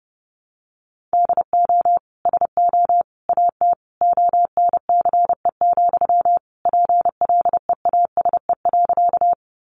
Morse code, heard as a keyed tone, about 30 wpm, 700 Hz. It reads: DO SO AT ONCE, PLEASE.
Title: Sentences from Top 1500 Words 30wpm